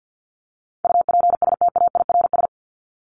Morse code, heard as a keyed tone, about 50 wpm, 700 Hz. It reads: UPSTAIRS
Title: Top 1751-2000 Words 50wpm